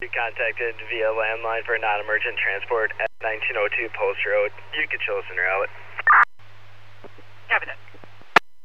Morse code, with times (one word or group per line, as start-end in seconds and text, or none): none